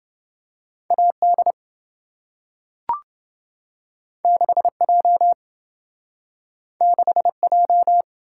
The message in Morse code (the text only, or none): AD 6J 6J